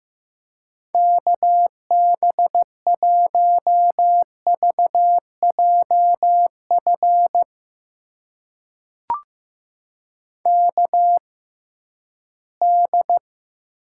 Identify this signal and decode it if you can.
Morse code, heard as a keyed tone, about 15 wpm, spaced as for 11 wpm, 700 Hz. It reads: KB1VJF KD
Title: USA Call Signs - 6 Characters - 6x Character Spacing - 15wpm